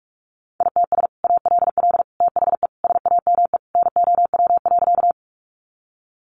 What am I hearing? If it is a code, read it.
Morse code, heard as a keyed tone, about 44 wpm, 700 Hz. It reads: ITS ALL THE SAME NOW.
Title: Sentences from Top 600 Words 45wpm